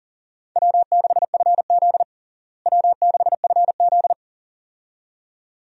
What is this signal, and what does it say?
Morse code, heard as a keyed tone, about 40 wpm, 700 Hz. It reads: W6FZ W6FZ
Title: USA Call Signs - 4 Characters - Repeated 2x - 40wpm